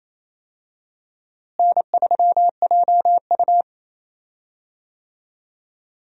Morse code, heard as a keyed tone, about 28 wpm, 700 Hz. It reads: N3JU
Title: USA Call Signs - 4 Characters - Repeated 3x - 28wpm